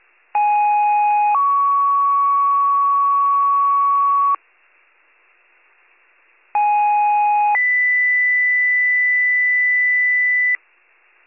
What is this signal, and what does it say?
Two-tone page, heard as a one-second tone, then a three-second tone, about 830 then 1120 Hz, about 830 then 2000 Hz. It is heard from 0.3 to 10.6 s.